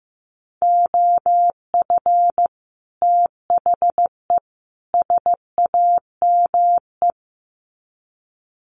Morse code, heard as a keyed tone, about 15 wpm, 700 Hz. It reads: OF THE SAME
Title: Sets of 3 Words 15wpm